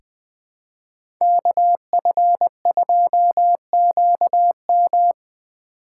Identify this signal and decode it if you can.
Morse code, heard as a keyed tone, about 20 wpm, 700 Hz. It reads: KF2QM
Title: USA Call Signs - 3x Character Spacing - 20wpm